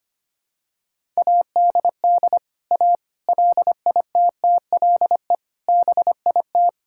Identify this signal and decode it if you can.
Morse code, heard as a keyed tone, about 25 wpm, 700 Hz. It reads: ADD A LITTLE BIT
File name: practice / 25wpm / Sentences with 4-Word Phrases - 25wpm